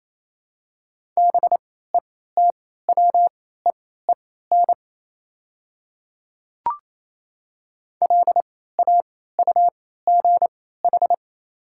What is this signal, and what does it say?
Morse code, heard as a keyed tone, about 28 wpm, spaced as for 16 wpm, 700 Hz. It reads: BETWEEN LAUGH
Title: 3x Character Spacing - Top 500 Words - 28wpm